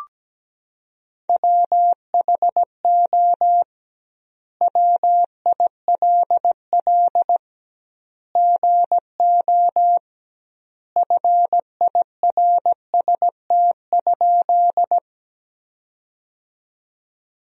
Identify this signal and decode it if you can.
Morse code, heard as a keyed tone, about 17 wpm, 700 Hz. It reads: WHO WILL GO FIRST?
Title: 2x Word-Spacing - Sentences from Top 100 Words - 17wpm